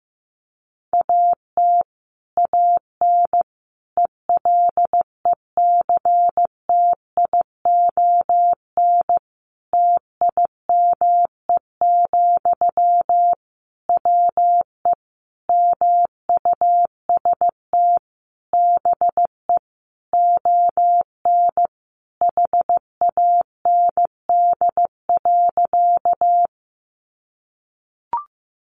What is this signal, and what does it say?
Morse code, heard as a keyed tone, about 15 wpm, 700 Hz. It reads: AT AN ELECTION TIME, WE MUST BE ON HAND.
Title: Sentences from Top 1500 Words 15wpm